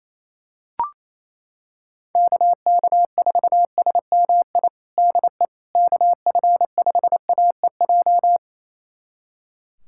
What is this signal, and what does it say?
Morse code, heard as a keyed tone, about 28 wpm, 700 Hz. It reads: KK4SMI DE KF5AEJ
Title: USA Callsign DE Callsign - 6 Characters - 28wpm